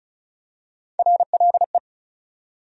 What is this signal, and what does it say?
Morse code, heard as a keyed tone, about 36 wpm, 700 Hz. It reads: RLE
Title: USA Call Sign Postfixes - 3 Letters 35wpm